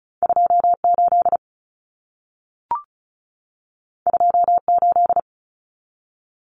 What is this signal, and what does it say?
Morse code, heard as a keyed tone, about 36 wpm, 700 Hz. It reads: 28 28
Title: Numbers - 2-Digit 35wpm